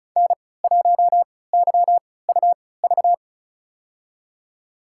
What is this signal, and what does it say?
Morse code, heard as a keyed tone, about 36 wpm, spaced as for 20 wpm, 700 Hz. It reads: N1YUV
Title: USA Call Signs - 3x Character Spacing - 35wpm